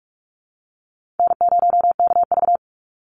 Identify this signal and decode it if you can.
Morse code, heard as a keyed tone, about 46 wpm, 700 Hz. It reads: N0KV
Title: Easy Call Signs - 45wpm